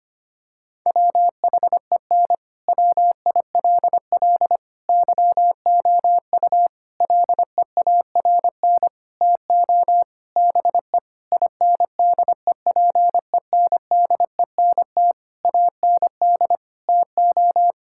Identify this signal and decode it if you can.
Morse code, heard as a keyed tone, about 25 wpm, 700 Hz. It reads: WHEN WILL YOU LEARN TO BE INDEPENDENT AND TO
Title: Sentences from Top 1000 Words - Review Part 6 - 25wpm